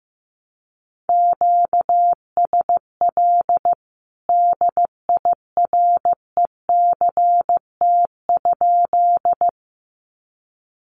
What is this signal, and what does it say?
Morse code, heard as a keyed tone, about 15 wpm, 700 Hz. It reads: QSL DIRECT?
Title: Common QSO Phrases 15wpm